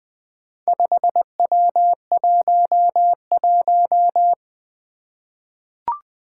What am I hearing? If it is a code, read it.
Morse code, heard as a keyed tone, about 20 wpm, 700 Hz. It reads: 5W11
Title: Four Challenging Letter-Number Combinations 20wpm